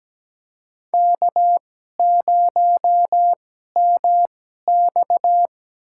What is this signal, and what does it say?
Morse code, heard as a keyed tone, about 17 wpm, spaced as for 12 wpm, 700 Hz. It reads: K0MX
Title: USA Call Signs - 4 Characters - 2x Character Spacing - 17wpm